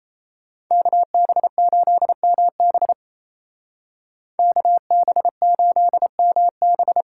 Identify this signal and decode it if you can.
Morse code, heard as a keyed tone, about 33 wpm, 700 Hz. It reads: KB8MB KB8MB
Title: Speed Racing - USA 2x2 Call Signs 28wpm